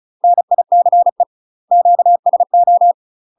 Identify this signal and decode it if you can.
Morse code, heard as a keyed tone, about 34 wpm, 700 Hz. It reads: NICE QSO
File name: UK MASTER QSOs TRAINING / 35WPM / UK MASTER QSOs 0003